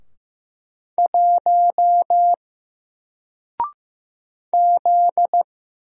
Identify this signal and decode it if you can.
Morse code, heard as a keyed tone, about 15 wpm, 700 Hz. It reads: 1 Z
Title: Single Letter-Number 15wpm